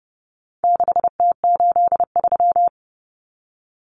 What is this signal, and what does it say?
Morse code, heard as a keyed tone, about 30 wpm, 700 Hz. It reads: 6T83
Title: Common Cut Numbers - 4-Digit 30wpm